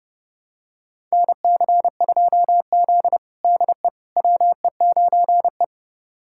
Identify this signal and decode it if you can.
Morse code, heard as a keyed tone, about 30 wpm, 700 Hz. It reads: NC2Z DE WE9E
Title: USA Callsign DE Callsign - 4 Characters - 30wpm